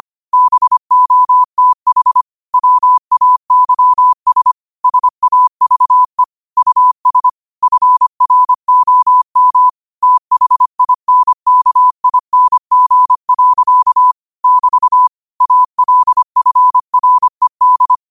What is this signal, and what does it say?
Morse code, heard as a keyed tone, about 25 wpm, 1000 Hz. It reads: BOTH WAYS SAVE US FROM THINKING. <BT> ALFRED
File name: qotdfiles / 25wpm